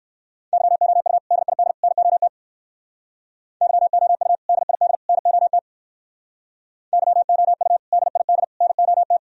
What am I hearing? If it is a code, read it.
Morse code, heard as a keyed tone, about 58 wpm, 700 Hz. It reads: YOU DID NOT YOU DID NOT YOU DID NOT